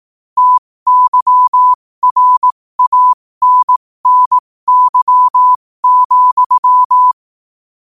Morse code, heard as a keyed tone, about 18 wpm, spaced as for 15 wpm, 1000 Hz. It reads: TYRANNY,
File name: qotdfiles / 15wpm